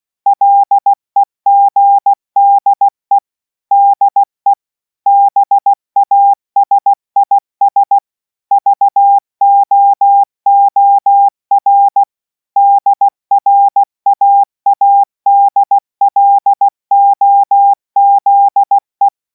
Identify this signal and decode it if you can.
Morse code, heard as a keyed tone, about 16 wpm, 800 Hz. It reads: LEGDE DE BASIS VOOR DRAADLOZE